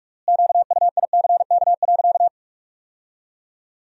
Morse code, heard as a keyed tone, about 46 wpm, 700 Hz. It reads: QUICK.
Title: 3x Word-Spacing - Sentences from Top 500 Words - 45wpm